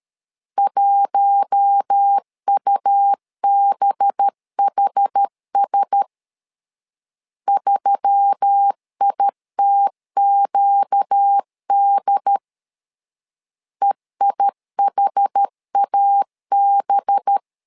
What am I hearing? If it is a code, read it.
Morse code, heard as a keyed tone, about 13 wpm, 800 Hz. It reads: 1UBHS 3ITQD EIHAB
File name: LEZIONI / 5 LBVC / 5 LBV 12wpm 1855